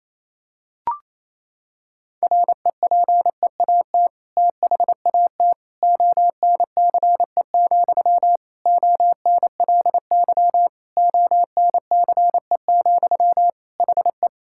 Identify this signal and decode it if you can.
Morse code, heard as a keyed tone, about 28 wpm, 700 Hz. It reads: REPEAT THAT ONCE, ONLY ONCE, HE
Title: Sentences from Top 1600 Words 28wpm